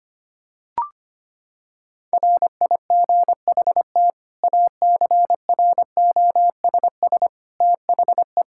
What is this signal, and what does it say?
Morse code, heard as a keyed tone, about 25 wpm, 700 Hz. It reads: RIGHT ACROSS THE
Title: Sentences from Top 1000 Words - Review Part 5 - 25wpm